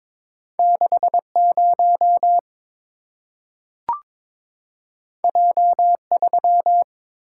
Morse code, heard as a keyed tone, about 22 wpm, 700 Hz. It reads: 60 J3
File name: practice / 22wpm / Two Letter-Number Combinations - 22wpm